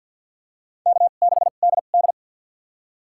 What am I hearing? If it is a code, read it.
Morse code, heard as a keyed tone, about 47 wpm, spaced as for 35 wpm, 700 Hz. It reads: K6DD
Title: USA Call Signs - 4 Characters - 2x Character Spacing - 50wpm